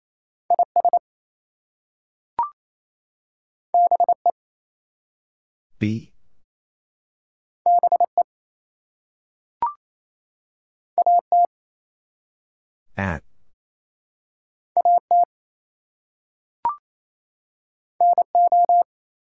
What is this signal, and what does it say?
Morse code, heard as a keyed tone, about 28 wpm, 700 Hz. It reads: IS BE BE AT AT NO
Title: Two Letter Words - 28wpm